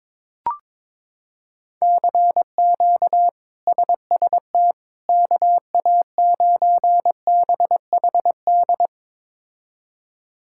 Morse code, heard as a keyed tone, about 22 wpm, 700 Hz. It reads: CQ SST KA9BHD
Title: K1USN SST QSO Practice - 22wpm